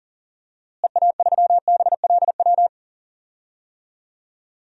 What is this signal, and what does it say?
Morse code, heard as a keyed tone, about 40 wpm, 700 Hz. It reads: EA3BLW
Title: DXCC-100 Call Signs - 6 Characters - Repeated 3x - 40wpm